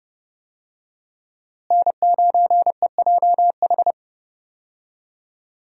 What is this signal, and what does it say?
Morse code, heard as a keyed tone, about 30 wpm, 700 Hz. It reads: N9EJH
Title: USA Call Signs - 5 Characters - Repeated 3x - 30wpm